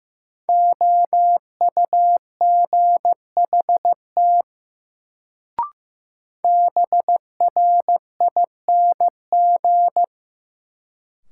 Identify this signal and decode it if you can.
Morse code, heard as a keyed tone, about 15 wpm, 700 Hz. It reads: OUGHT BRING
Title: Sets of 5 Words Encore - New Word Drill - 15wpm